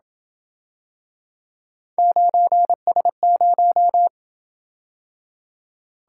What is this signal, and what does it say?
Morse code, heard as a keyed tone, about 27 wpm, 700 Hz. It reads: 9S0